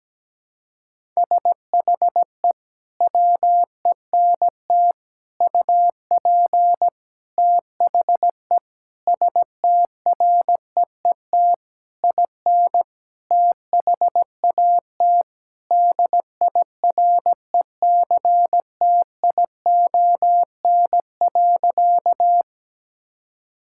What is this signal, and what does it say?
Morse code, heard as a keyed tone, about 17 wpm, 700 Hz. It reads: SHE WENT UP THE STREET IN THAT DIRECTION.
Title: Sentences from Top 1000 Words - Review Part 5 - 17wpm